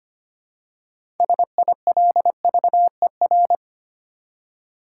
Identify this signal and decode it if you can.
Morse code, heard as a keyed tone, about 25 wpm, 700 Hz. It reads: SILVER
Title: Six Letter Words - 25wpm